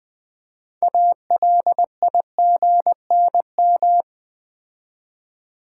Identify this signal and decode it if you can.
Morse code, heard as a keyed tone, about 20 wpm, 700 Hz. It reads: ALIGNM
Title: Build A Word - 3 Letters At A Time - 9-Letter Words - 20wpm